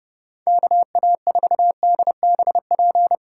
Morse code, heard as a keyed tone, about 31 wpm, 700 Hz. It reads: KA4DBP
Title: USA Call Signs - 6 Characters - 3x Character Spacing - 30wpm